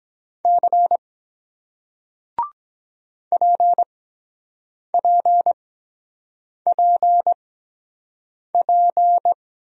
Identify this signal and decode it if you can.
Morse code, heard as a keyed tone, about 23 wpm, spaced as for 7 wpm, 700 Hz. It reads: C PPPP